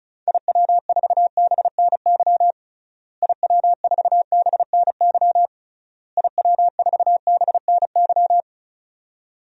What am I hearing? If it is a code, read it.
Morse code, heard as a keyed tone, about 36 wpm, 700 Hz. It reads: IW4BNY IW4BNY IW4BNY
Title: DXCC-100 Call Signs - 6 Characters - Repeated 3x - 35wpm